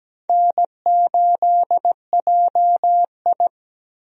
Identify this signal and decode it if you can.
Morse code, heard as a keyed tone, about 17 wpm, 700 Hz. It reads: N8JI
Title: USA Call Signs - 4 Characters - Repeated 3x - 17wpm